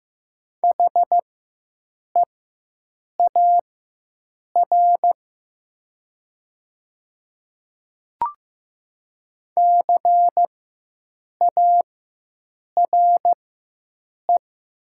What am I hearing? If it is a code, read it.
Morse code, heard as a keyed tone, about 15 wpm, spaced as for 7 wpm, 700 Hz. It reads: HEAR CARE
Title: 4x Character Spacing - Top 500 Words - 15wpm